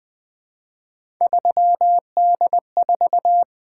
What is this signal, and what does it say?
Morse code, heard as a keyed tone, about 20 wpm, 700 Hz. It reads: 3D4
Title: Three Letter-Number Combinations - 20wpm